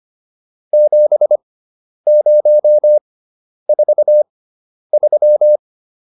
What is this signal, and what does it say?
Morse code, heard as a keyed tone, about 25 wpm, spaced as for 10 wpm, 600 Hz. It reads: 7043